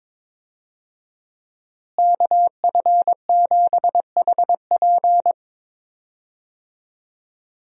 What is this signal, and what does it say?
Morse code, heard as a keyed tone, about 22 wpm, 700 Hz. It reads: KF7HP
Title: USA Call Signs - 5 Characters - Repeated 3x - 22wpm